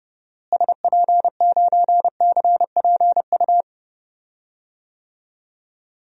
Morse code, heard as a keyed tone, about 30 wpm, 700 Hz. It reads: SP9CPU
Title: DXCC-100 Call Signs - 6 Characters - Repeated 3x - 30wpm